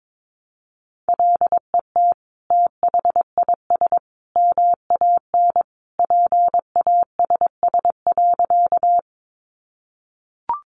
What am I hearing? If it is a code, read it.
Morse code, heard as a keyed tone, about 22 wpm, 700 Hz. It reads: LET THIS MAN PASS.
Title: Sentences from Top 400 Words 22wpm